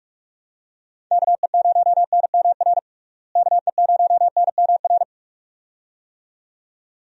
Morse code, heard as a keyed tone, about 43 wpm, 700 Hz. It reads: KE0NMR KE0NMR
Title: USA Call Signs - 6 Characters - Repeated 2x - 45wpm